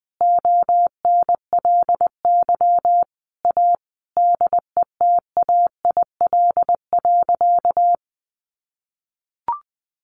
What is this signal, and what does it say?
Morse code, heard as a keyed tone, about 20 wpm, 700 Hz. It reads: ONLY A DETAIL.
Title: Sentences from Top 1500 Words 20wpm